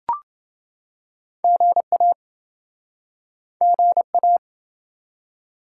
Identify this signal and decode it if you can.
Morse code, heard as a keyed tone, about 26 wpm, 700 Hz. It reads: GA GA